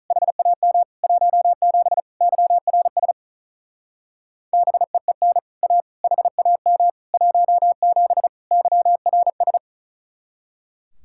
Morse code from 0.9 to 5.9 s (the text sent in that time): 17 YRS BEEN A